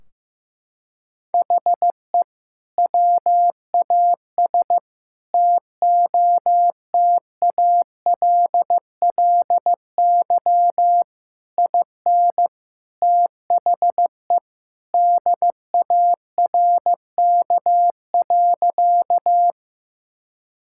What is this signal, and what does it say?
Morse code, heard as a keyed tone, about 15 wpm, 700 Hz. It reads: HE WAS TOTALLY IN THE DARK.
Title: Sentences from Top 1600 Words 15wpm